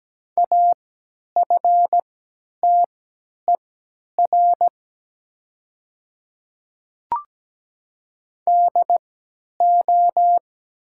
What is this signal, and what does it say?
Morse code, heard as a keyed tone, about 17 wpm, spaced as for 10 wpm, 700 Hz. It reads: AFTER DO
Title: 3x Character Spacing - Top 500 Words - 17wpm